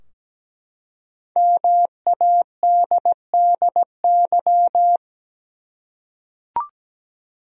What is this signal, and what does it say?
Morse code, heard as a keyed tone, about 17 wpm, 700 Hz. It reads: MADDY
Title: USA Common Names - 17wpm